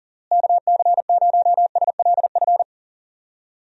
Morse code, heard as a keyed tone, about 40 wpm, 700 Hz. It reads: KC0SLF
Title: USA Call Signs - 6 Characters - Repeated 3x - 40wpm